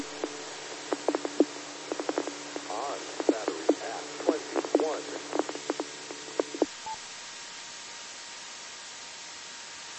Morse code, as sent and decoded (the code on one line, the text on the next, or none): none